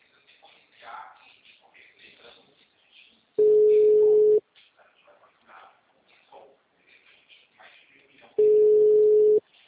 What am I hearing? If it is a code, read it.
Ringback tone (Europe, 425 Hz), with speech.